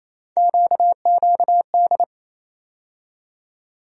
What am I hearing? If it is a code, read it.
Morse code, heard as a keyed tone, about 28 wpm, 700 Hz. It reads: QQD
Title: USA Call Sign Postfixes - 3 Letters 28wpm